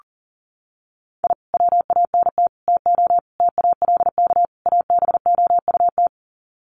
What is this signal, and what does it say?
Morse code, heard as a keyed tone, about 40 wpm, 700 Hz. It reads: I WANT TO TALK ABOUT
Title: Sets of 5 Words Encore 40wpm